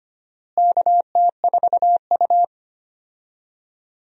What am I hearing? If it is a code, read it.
Morse code, heard as a keyed tone, about 25 wpm, 700 Hz. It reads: KT4U
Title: USA Call Signs - 4 Characters - Repeated 3x - 25wpm